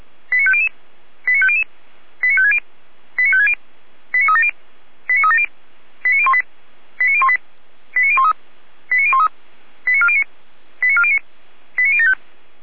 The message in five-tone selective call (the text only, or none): 89500 89500 89560 89560 89370 89370 89018 89018 89013 89013 89409 89409 89075